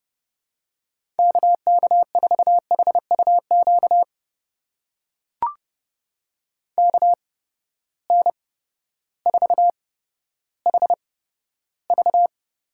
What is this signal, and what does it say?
Morse code, heard as a keyed tone, about 30 wpm, spaced as for 14 wpm, 700 Hz. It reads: KK4HUQ KN4HV
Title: USA Call Signs - 6 Characters - 8x Character Spacing - 30wpm